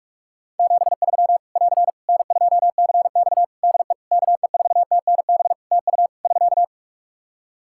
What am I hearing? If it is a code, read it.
Morse code, heard as a keyed tone, about 41 wpm, 700 Hz. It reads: 73 <AR> N2CX DE KE4TNB TU <SK>
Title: Practice QSO - Exchange 4 - Ending - 45wpm